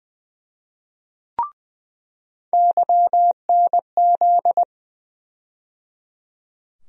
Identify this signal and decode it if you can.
Morse code, heard as a keyed tone, about 20 wpm, 700 Hz. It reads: YNZ
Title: USA Call Sign Postfixes - 3 Letters 20wpm